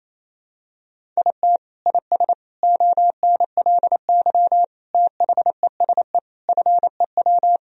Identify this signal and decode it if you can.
Morse code, heard as a keyed tone, about 28 wpm, 700 Hz. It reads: IT IS ONLY THESE FEW
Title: Sentences from Top 1000 Words - Review Part 1 - 28wpm